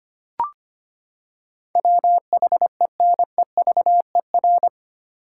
Morse code, heard as a keyed tone, about 25 wpm, 700 Hz. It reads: WHENEVER
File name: Eight Letter Words - 25wpm